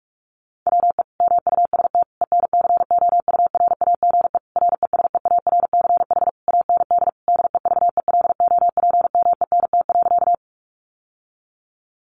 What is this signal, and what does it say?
Morse code, heard as a keyed tone, about 44 wpm, 700 Hz. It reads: WE MUST ENCOURAGE RESEARCH AND DEVELOPMENT.
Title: Sentences from Top 1400 Words 45wpm